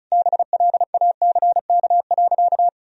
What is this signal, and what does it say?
Morse code, heard as a keyed tone, about 34 wpm, 700 Hz. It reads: BLACK.
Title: Sentences from Top 1000 Words - Review Part 2 - 35wpm